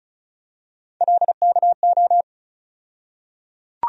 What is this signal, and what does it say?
Morse code, heard as a keyed tone, about 36 wpm, 700 Hz. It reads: LKO
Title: USA Call Sign Postfixes - 3 Letters 35wpm